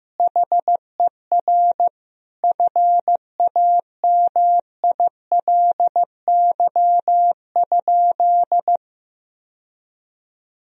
Morse code, heard as a keyed tone, about 15 wpm, 700 Hz. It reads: HER FAMILY?
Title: Sentences from Top 1000 Words - Review Part 4 - 15wpm